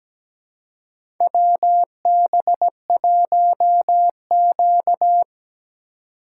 Morse code, heard as a keyed tone, about 17 wpm, 700 Hz. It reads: WB1Q
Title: USA Call Signs - 3x Character Spacing - 17wpm